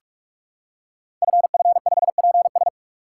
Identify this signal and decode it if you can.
Morse code, heard as a keyed tone, about 44 wpm, 690 Hz. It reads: FF5PS
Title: Code Groups - 5 Letter:Numbers - 45wpm